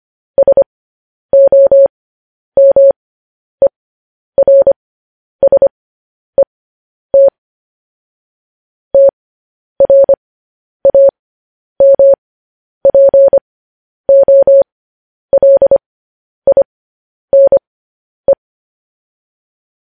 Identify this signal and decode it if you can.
Morse code, heard as a keyed tone, about 25 wpm, spaced as for 10 wpm, 550 Hz. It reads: SOMERSET TRAMPOLINE
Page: News Headlines in Morse Code at 10 WPM